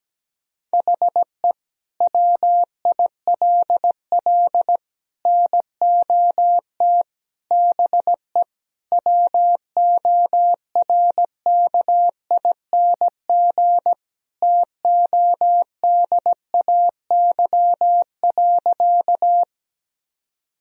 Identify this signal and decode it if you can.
Morse code, heard as a keyed tone, about 17 wpm, 700 Hz. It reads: HE WILL NOT BE WORKING TODAY.
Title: Sentences from Top 1000 Words - Review Part 4 - 17wpm